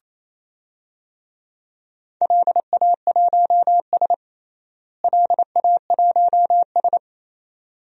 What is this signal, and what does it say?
Morse code, heard as a keyed tone, about 27 wpm, 700 Hz. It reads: LA1S LA1S
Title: DXCC-100 Call Signs - 4 Characters - Repeated 2x - 28wpm